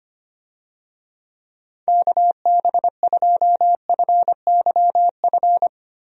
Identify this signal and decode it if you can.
Morse code, heard as a keyed tone, about 25 wpm, 700 Hz. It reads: KB2FYF
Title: USA Call Signs - 6 Characters - Repeated 3x - 25wpm